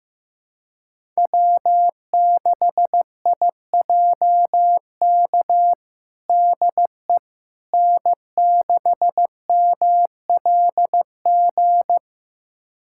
Morse code, heard as a keyed tone, about 16 wpm, 700 Hz. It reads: W6IJK DE N6MLG
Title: USA Callsign DE Callsign - 5 Characters - 15wpm